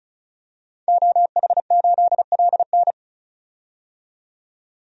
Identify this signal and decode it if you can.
Morse code, heard as a keyed tone, about 34 wpm, 700 Hz. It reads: OH8LN
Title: DXCC-100 Call Signs - 5 Characters - 35wpm